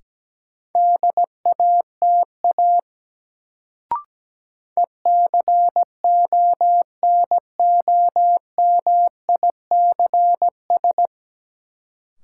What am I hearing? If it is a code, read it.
Morse code, heard as a keyed tone, about 17 wpm, 700 Hz. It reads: DATA ECONOMICS
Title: Sentences from Top 1000 Words - Review Part 3 - New Word Drill - 17wpm